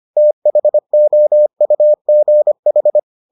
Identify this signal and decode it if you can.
Morse code, heard as a keyed tone, about 25 wpm, 600 Hz. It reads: THOUGH